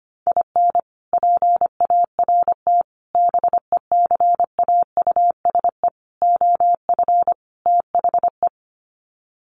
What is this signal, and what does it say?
Morse code, heard as a keyed tone, about 25 wpm, 700 Hz. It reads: IN PART BECAUSE OF THE
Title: Sets of 5 Words Encore2 25wpm